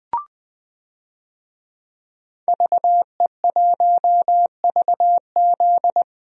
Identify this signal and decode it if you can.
Morse code, heard as a keyed tone, about 20 wpm, 700 Hz. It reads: VE1VZ
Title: DXCC-100 Call Signs - 5 Characters - Repeated 3x - 20wpm